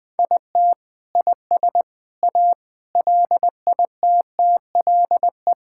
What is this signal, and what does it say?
Morse code, heard as a keyed tone, about 21 wpm, 700 Hz. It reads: IT IS A LITTLE
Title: Sentences from Top 1000 Words - Review Part 5 - 20wpm